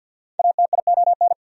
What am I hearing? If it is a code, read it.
Morse code, heard as a keyed tone, about 53 wpm, 690 Hz. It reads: ATION
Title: Prefixes and Suffixes 50wpm